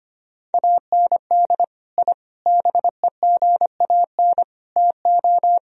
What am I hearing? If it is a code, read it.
Morse code, heard as a keyed tone, about 25 wpm, 700 Hz. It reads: AND I BEGAN TO
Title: Sentences from Top 1000 Words - Review Part 2 - 25wpm